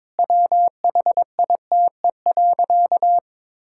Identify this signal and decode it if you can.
Morse code, heard as a keyed tone, about 22 wpm, 700 Hz. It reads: WHITE.
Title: Sentences from Top 1000 Words - Review Part 6 - 22wpm